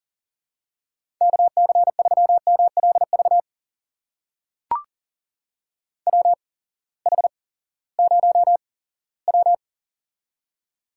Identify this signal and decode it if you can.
Morse code, heard as a keyed tone, about 40 wpm, spaced as for 20 wpm, 700 Hz. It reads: KC3MLV WH0W
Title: USA Call Signs - 8x Character Spacing - 40wpm